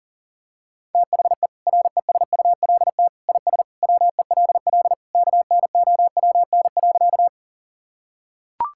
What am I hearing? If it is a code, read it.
Morse code, heard as a keyed tone, about 40 wpm, 700 Hz. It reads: THE RESULT IS WELL KNOWN.
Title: Sentences from Top 1000 Words - Review Part 5 - 40wpm